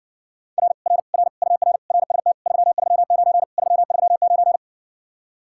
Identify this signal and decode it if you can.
Morse code, heard as a keyed tone, about 60 wpm, 700 Hz. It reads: R R R UR RST 339 339